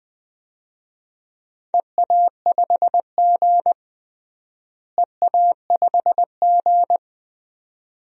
Morse code, heard as a keyed tone, about 20 wpm, 700 Hz. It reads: EA5G EA5G
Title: DXCC-100 Call Signs - 4 Characters - Repeated 2x - 20wpm